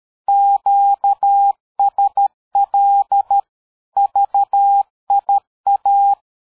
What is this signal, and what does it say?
Morse code, heard as a keyed tone, about 13 wpm, 800 Hz. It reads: QSL VIA
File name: dk5ke / QSO 60BpM